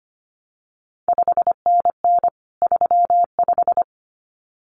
Morse code, heard as a keyed tone, about 25 wpm, 700 Hz. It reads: 5NN 35
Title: CQ WW DX Contest Exchange Encore - 25wpm